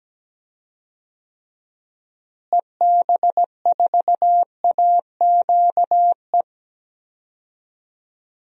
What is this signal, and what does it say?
Morse code, heard as a keyed tone, about 17 wpm, 700 Hz. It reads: EB4AQE
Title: DXCC-100 Call Signs - 6 Characters - Repeated 2x - 17wpm